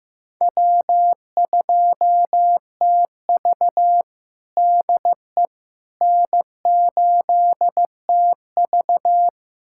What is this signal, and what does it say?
Morse code, heard as a keyed tone, about 16 wpm, 700 Hz. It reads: W2TV DE N8TV
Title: USA Callsign DE Callsign - 4 Characters - 15wpm